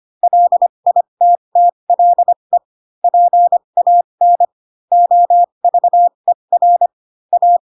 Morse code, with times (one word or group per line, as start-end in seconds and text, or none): 0.2-2.6 LITTLE
3.0-4.5 PAN
4.9-6.9 OVER
7.3-7.6 A